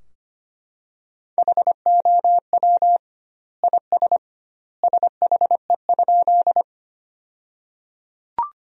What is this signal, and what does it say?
Morse code, heard as a keyed tone, about 25 wpm, 700 Hz. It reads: HOW IS SHE?
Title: 2x Word-Spacing - Sentences from Top 100 Words - 25wpm